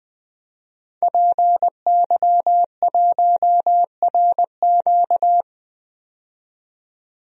Morse code, heard as a keyed tone, about 20 wpm, 700 Hz. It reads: PY1RQ
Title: DXCC-100 Call Signs - 5 Characters - 20wpm